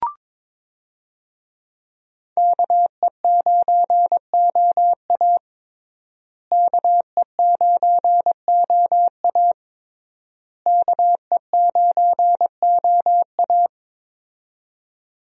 Morse code, heard as a keyed tone, about 22 wpm, 700 Hz. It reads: KE9OA KE9OA KE9OA
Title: USA Call Signs - 5 Characters - Repeated 3x - 22wpm